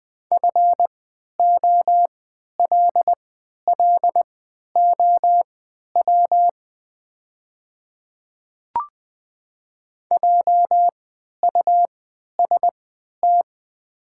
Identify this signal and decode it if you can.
Morse code, heard as a keyed tone, about 20 wpm, spaced as for 11 wpm, 700 Hz. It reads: FOLLOW JUST
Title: 3x Character Spacing - Top 200 Words - 20wpm